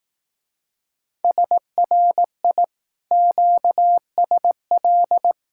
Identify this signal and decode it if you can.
Morse code, heard as a keyed tone, about 18 wpm, 700 Hz. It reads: SRI QSL
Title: Speed-Racing Common QSO Phrases Encore - 15wpm